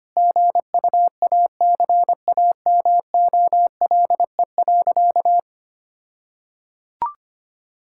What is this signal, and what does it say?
Morse code, heard as a keyed tone, about 25 wpm, 700 Hz. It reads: GUACAMOLE.
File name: practice / 25wpm / All-Age Appropriate Jokes - 25wpm